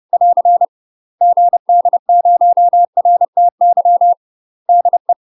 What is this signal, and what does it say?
Morse code, heard as a keyed tone, about 30 wpm, 700 Hz. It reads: <AR> GD0RTY DE
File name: UK MASTER QSOs TRAINING / 30WPM / UK MASTER QSOs 0004